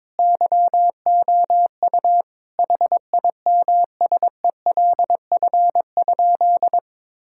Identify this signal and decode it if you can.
Morse code, heard as a keyed tone, about 22 wpm, 700 Hz. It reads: YOU HIMSELF?
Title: Sentences from Top 1600 Words 22wpm